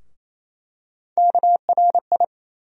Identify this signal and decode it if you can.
Morse code, heard as a keyed tone, about 28 wpm, 700 Hz. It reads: KRI
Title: USA Call Sign Postfixes - 3 Letters 28wpm